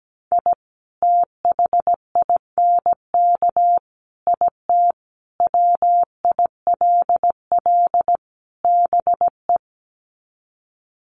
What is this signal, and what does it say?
Morse code, heard as a keyed tone, about 17 wpm, 700 Hz. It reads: I THINK IT WILL BE
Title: Sets of 5 Words Encore2 17wpm